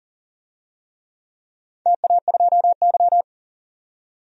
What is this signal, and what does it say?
Morse code, heard as a keyed tone, about 37 wpm, 700 Hz. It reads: TA2Y
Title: DXCC-100 Call Signs - 4 Characters - 40wpm